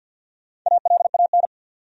Morse code, heard as a keyed tone, about 48 wpm, 700 Hz. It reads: ALAN
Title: USA Common Names - Max 4 Letters - 50wpm